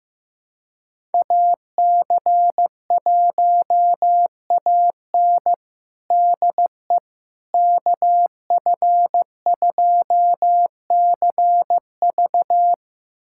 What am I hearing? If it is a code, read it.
Morse code, heard as a keyed tone, about 15 wpm, 700 Hz. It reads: AC1AN DE KF2CV
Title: USA Callsign DE Callsign - 5 Characters - 15wpm